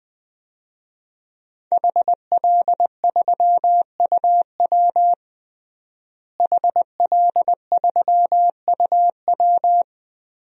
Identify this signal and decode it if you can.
Morse code, heard as a keyed tone, about 20 wpm, 700 Hz. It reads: HL3UW HL3UW
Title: DXCC-100 Call Signs - 5 Characters - Repeated 2x - 20wpm